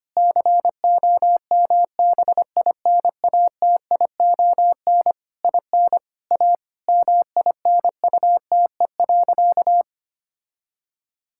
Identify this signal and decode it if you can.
Morse code, heard as a keyed tone, about 25 wpm, 700 Hz. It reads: COMBINATION IN A MINUTE.
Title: Sentences from Top 1000 Words - Review Part 6 - 25wpm